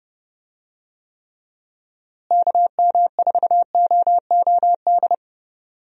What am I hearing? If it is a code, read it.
Morse code, heard as a keyed tone, about 30 wpm, 700 Hz. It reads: KM4OOD
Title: USA Call Signs - 6 Characters - Repeated 3x - 30wpm